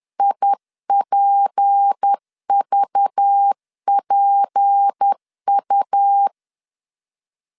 Morse code, heard as a keyed tone, about 10 wpm, 800 Hz. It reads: IPVPU
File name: LEZIONI / 6 PKX0 / 6 P 10wpm 2432